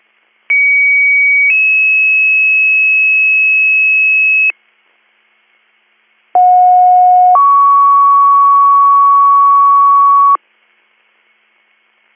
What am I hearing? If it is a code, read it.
Two-tone page, heard as a one-second tone, then a three-second tone, about 2250 then 2600 Hz, about 730 then 1120 Hz. It is heard from 0.5 to 10.4 s.